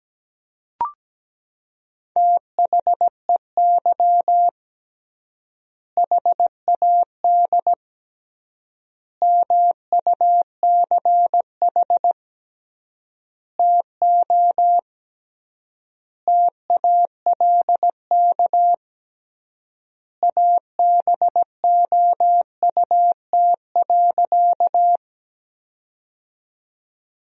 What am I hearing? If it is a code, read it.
Morse code, heard as a keyed tone, about 17 wpm, 700 Hz. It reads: THEY HAD MUCH TO TALK ABOUT.
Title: 3x Word-Spacing - Sentences from Top 400 Words - 17wpm